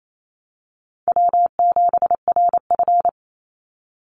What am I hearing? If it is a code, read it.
Morse code, heard as a keyed tone, about 28 wpm, 700 Hz. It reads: W7RF
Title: Call Signs - 28wpm